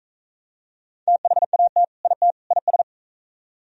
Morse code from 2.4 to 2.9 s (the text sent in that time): IS